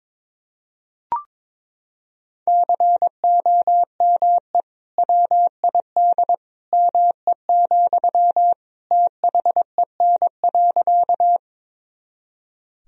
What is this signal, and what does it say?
Morse code, heard as a keyed tone, about 22 wpm, 700 Hz. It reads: COME WID ME, THEN.
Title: Sentences from Top 200 Words and Abbreviations 22wpm